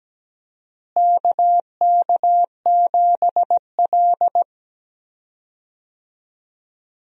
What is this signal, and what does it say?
Morse code, heard as a keyed tone, about 17 wpm, 700 Hz. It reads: KK7L
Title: USA Call Signs - 4 Characters - Repeated 3x - 17wpm